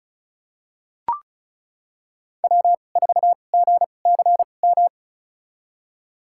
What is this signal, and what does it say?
Morse code, heard as a keyed tone, about 34 wpm, spaced as for 25 wpm, 700 Hz. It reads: W4GCM
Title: USA Call Signs - 2x Character Spacing - 35wpm